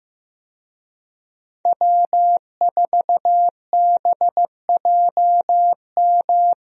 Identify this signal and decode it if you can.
Morse code, heard as a keyed tone, about 15 wpm, 700 Hz. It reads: W4BJM
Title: USA Call Signs - 5 Characters - Repeated 3x - 15wpm